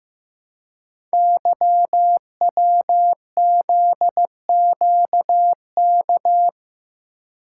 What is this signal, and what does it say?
Morse code, heard as a keyed tone, about 15 wpm, 700 Hz. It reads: YWZQK
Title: Code Groups - 5 Letters - 15wpm